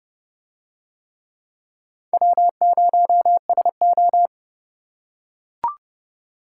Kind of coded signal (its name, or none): Morse code